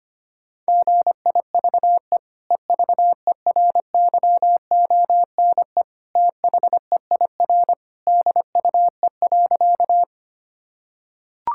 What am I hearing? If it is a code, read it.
Morse code, heard as a keyed tone, about 25 wpm, 700 Hz. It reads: GIVE EVERYONE THEIR DUE.
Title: Sentences from Top 1000 Words - Review Part 6 - 25wpm